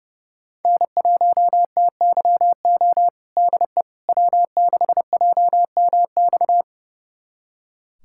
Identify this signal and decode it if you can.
Morse code, heard as a keyed tone, about 30 wpm, 700 Hz. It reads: N1TYO DE W6JMX
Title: USA Callsign DE Callsign - 5 Characters - 30wpm